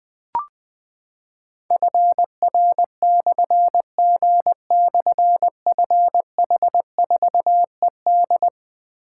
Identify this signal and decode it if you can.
Morse code, heard as a keyed tone, about 20 wpm, 700 Hz. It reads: FR/G/FH4ED
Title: Difficult Call Signs 20wpm